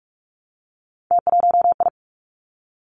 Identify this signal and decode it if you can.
Morse code, heard as a keyed tone, about 46 wpm, 700 Hz. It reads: T1I
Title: Three Challenging Letter-Number Combinations 45wpm